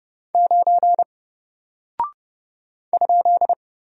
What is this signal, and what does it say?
Morse code, heard as a keyed tone, about 30 wpm, 700 Hz. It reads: 9 ?
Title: Single Character (Letter:Number:Punctuation:Prosign) Alt 30wpm